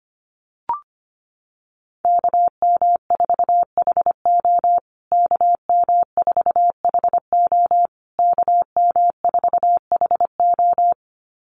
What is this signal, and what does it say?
Morse code, heard as a keyed tone, about 25 wpm, 700 Hz. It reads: KM4HO KM4HO KM4HO
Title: USA Call Signs Repeated 3x 25wpm